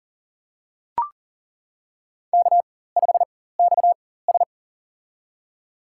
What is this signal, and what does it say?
Morse code, heard as a keyed tone, about 40 wpm, spaced as for 19 wpm, 700 Hz. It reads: K5XS
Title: USA Call Signs - 4 Characters - 4x Character Spacing - 40wpm